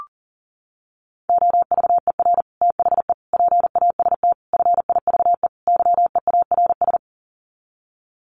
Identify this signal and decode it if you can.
Morse code, heard as a keyed tone, about 40 wpm, 700 Hz. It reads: OVER THE PAST FIVE YEARS
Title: Sets of 5 Words Encore 40wpm